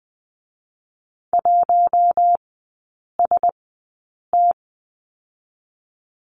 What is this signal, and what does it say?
Morse code, heard as a keyed tone, about 21 wpm, spaced as for 9 wpm, 690 Hz. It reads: 1ST
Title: 087 Lesson 15e - Number 5 - Three Characters 20wpm